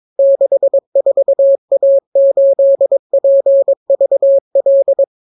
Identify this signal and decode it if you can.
Morse code, heard as a keyed tone, about 22 wpm, 550 Hz. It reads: 64A8PVL